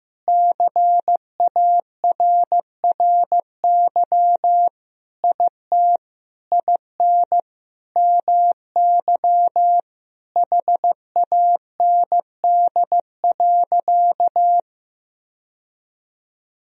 Morse code, heard as a keyed tone, about 15 wpm, 700 Hz. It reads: CARRY IT IN MY HAND.
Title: Sentences from Top 1000 Words - Review Part 1 - 15wpm